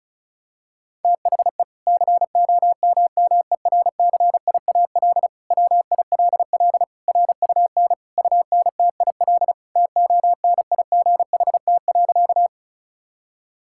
Morse code, heard as a keyed tone, about 36 wpm, 700 Hz. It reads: THE COMMERCIAL WILL RUN UNTIL TONIGHT.
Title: Sentences from Top 1000 Words - Review Part 5 - 35wpm